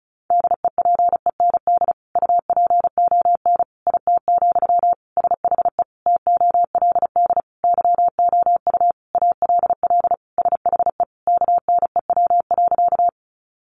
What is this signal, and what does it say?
Morse code, heard as a keyed tone, about 34 wpm, 700 Hz. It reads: DEPEND UPON IT, SHE TOLD YOU ALL SHE KNEW.
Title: Sentences from Top 1300 Words 35wpm